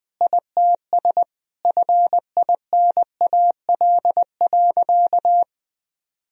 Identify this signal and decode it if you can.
Morse code, heard as a keyed tone, about 20 wpm, 700 Hz. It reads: ITS FINAL.
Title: Sentences from Top 1000 Words - Review Part 2 - 20wpm